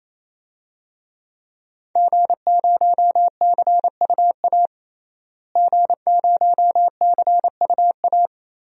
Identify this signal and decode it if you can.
Morse code, heard as a keyed tone, about 28 wpm, 700 Hz. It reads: G0CUA G0CUA
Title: DXCC-100 Call Signs - 5 Characters - Repeated 3x - 28wpm